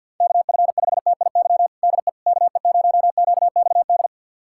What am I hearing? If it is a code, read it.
Morse code, heard as a keyed tone, about 48 wpm, 700 Hz. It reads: KF5TIY DE KE0QXD
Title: USA Callsign DE Callsign - 6 Characters - 50wpm